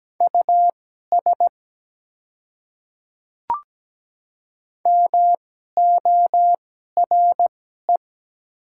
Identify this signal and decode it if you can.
Morse code, heard as a keyed tone, about 17 wpm, spaced as for 12 wpm, 700 Hz. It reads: US MORE
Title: 2x Character Spacing - Top 200 Words - 17wpm